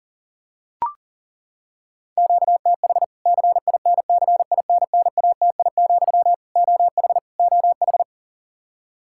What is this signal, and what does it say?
Morse code, heard as a keyed tone, about 40 wpm, 700 Hz. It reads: QTH CINCINNATI, OH OH
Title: K1USN SST QSO Practice 40wpm